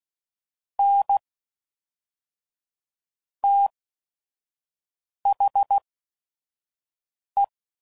Morse code, heard as a keyed tone, about 16 wpm, spaced as for 5 wpm, 800 Hz. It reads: N THE
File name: CW Code Groups @5WPM / Technical Subjects / XC